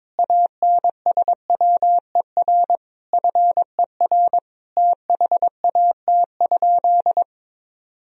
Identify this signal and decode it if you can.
Morse code, heard as a keyed tone, about 21 wpm, 700 Hz. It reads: ANSWER FER THAT?
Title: Sentences from Top 300 Words and Abbreviations 22wpm